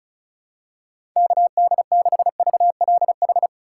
Morse code, heard as a keyed tone, about 36 wpm, 700 Hz. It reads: KD6VLH
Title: USA Call Signs - 6 Characters - 4x Character Spacing - 35wpm